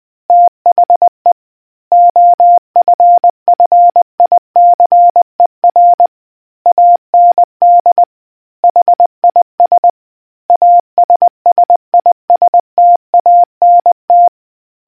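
Morse code, heard as a keyed tone, about 20 wpm, 700 Hz. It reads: THE OFFICER AND HIS ASSISTANT
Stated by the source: Morse code history quiz question from a New Zealand naval communicators association.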